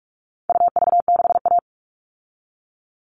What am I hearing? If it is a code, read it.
Morse code, heard as a keyed tone, about 44 wpm, 700 Hz. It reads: UV6A
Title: All Cut Numbers - 4-Digit 45wpm